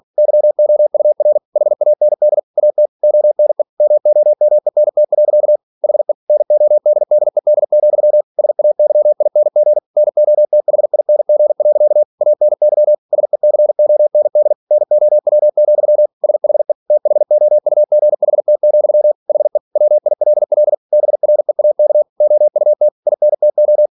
Morse code, heard as a keyed tone, about 46 wpm, 600 Hz. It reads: YOUR HAND AT ONE MOMENT. HE NODDED, SAYING NOTHING. ANY SECOND NOW, SHE THOUGHT, HE WILL BREAK OUT INTO
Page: Never 47wpm